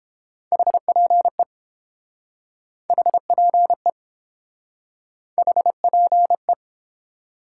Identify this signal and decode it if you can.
Morse code, heard as a keyed tone, about 32 wpm, 700 Hz. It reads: HPE HPE HPE